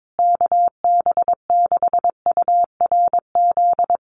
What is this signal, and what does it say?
Morse code, heard as a keyed tone, about 22 wpm, 700 Hz. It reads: KB6URZ
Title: USA Call Signs Repeated 3x 22wpm